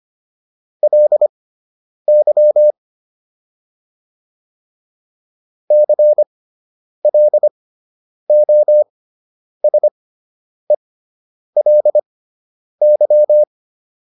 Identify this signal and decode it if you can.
Morse code, heard as a keyed tone, about 25 wpm, spaced as for 9 wpm, 600 Hz. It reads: LY CLOSELY
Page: fundamental-practice-files